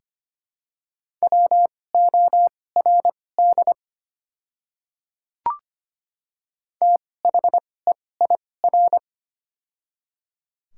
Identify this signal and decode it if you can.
Morse code, heard as a keyed tone, about 25 wpm, spaced as for 18 wpm, 700 Hz. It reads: WORD THEIR
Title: 2x Character Spacing - Top 100 Words - 25wpm